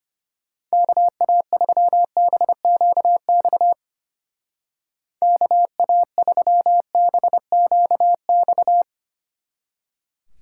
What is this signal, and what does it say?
Morse code, heard as a keyed tone, about 30 wpm, 700 Hz. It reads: KA3BQX KA3BQX
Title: Speed Racing - USA 2x3 Call Signs 25wpm